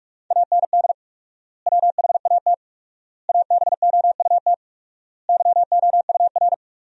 Morse code, heard as a keyed tone, about 47 wpm, 700 Hz. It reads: AND WHAT ABOUT YOUR
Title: 4x Word-Spacing - Sentences from Top 200 Words - 45wpm